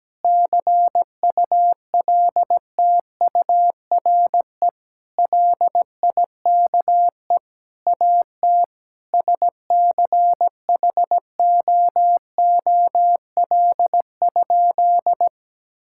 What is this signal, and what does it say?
Morse code, heard as a keyed tone, about 17 wpm, 700 Hz. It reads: CULTURE LIKE AT SCHOOL?
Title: Sentences from Top 1000 Words - Review Part 6 - 17wpm